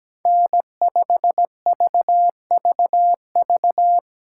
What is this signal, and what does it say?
Morse code, heard as a keyed tone, about 17 wpm, 700 Hz. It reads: N5VVV
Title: USA Call Signs - 5 Characters - 6x Character Spacing - 17wpm